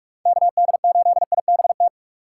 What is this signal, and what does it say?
Morse code, heard as a keyed tone, about 46 wpm, 700 Hz. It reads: KD8IBT
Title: USA Call Signs - 6 Characters - 6x Character Spacing - 45wpm